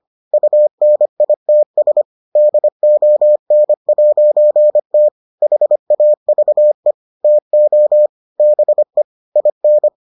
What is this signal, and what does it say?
Morse code, heard as a keyed tone, about 25 wpm, 600 Hz. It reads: UNITS DON'T HAVE TO BE IN